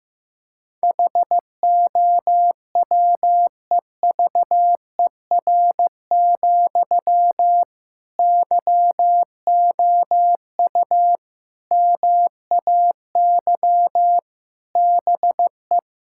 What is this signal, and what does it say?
Morse code, heard as a keyed tone, about 15 wpm, 700 Hz. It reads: HOWEVER, YOU MAY BE
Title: Sentences from Top 1600 Words 15wpm